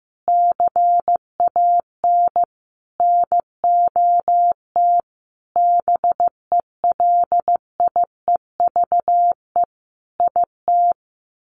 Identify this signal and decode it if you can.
Morse code, heard as a keyed tone, about 15 wpm, 700 Hz. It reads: CAN NOT BELIEVE IT
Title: Sets of 5 Words Encore 15wpm